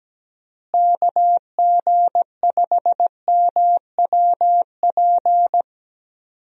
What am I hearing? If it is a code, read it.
Morse code, heard as a keyed tone, about 17 wpm, 700 Hz. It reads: KG5MWP
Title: USA Call Signs - 6 Characters - Repeated 3x - 17wpm